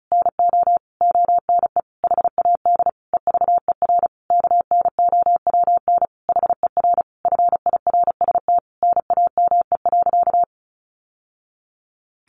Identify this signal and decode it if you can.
Morse code, heard as a keyed tone, about 34 wpm, 700 Hz. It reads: NO ONE HAD EVER KNOWN HER FIRST NAME.
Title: Sentences from Top 1200 Words 35wpm